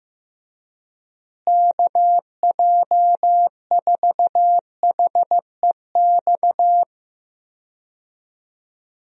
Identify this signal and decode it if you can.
Morse code, heard as a keyed tone, about 15 wpm, 700 Hz. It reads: KJ4HEX
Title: USA Call Signs - 6 Characters - Repeated 2x - 15wpm